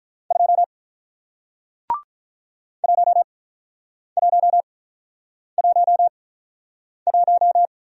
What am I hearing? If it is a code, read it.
Morse code, heard as a keyed tone, about 46 wpm, spaced as for 9 wpm, 700 Hz. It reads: 2 1111